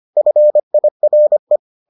Morse code, heard as a keyed tone, about 25 wpm, 600 Hz. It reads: FIRE